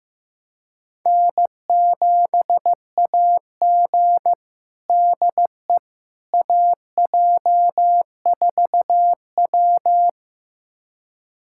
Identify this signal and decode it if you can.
Morse code, heard as a keyed tone, about 15 wpm, 700 Hz. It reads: N7AG DE AJ4W
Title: USA Callsign DE Callsign - 4 Characters - 15wpm